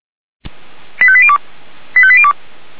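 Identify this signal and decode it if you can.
Five-tone selective call (ZVEI), heard as quick runs of five tones, tones of about 70 ms. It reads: 85802 85802
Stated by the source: ZVEI five-tone selective call received on SDR.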